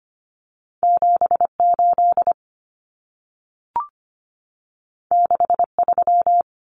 Morse code, heard as a keyed tone, about 25 wpm, 700 Hz. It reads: 78 63
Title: Numbers - 2-Digit 25wpm